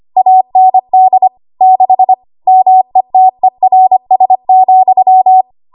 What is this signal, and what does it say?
Morse code, heard as a keyed tone, about 25 wpm, 750 Hz. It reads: AND 6 METERS,
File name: Archive / 25 WPM / 25WPM